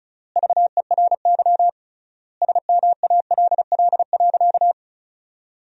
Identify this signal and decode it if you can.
Morse code, heard as a keyed tone, about 36 wpm, 700 Hz. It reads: VERY SMALL.
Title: 3x Word-Spacing - Sentences from Top 400 Words - 35wpm